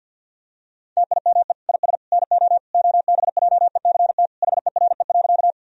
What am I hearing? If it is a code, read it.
Morse code, heard as a keyed tone, about 52 wpm, 700 Hz. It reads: TIME IS NO OBJECT HERE.
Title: Sentences from Top 1000 Words - Review Part 3 - 50wpm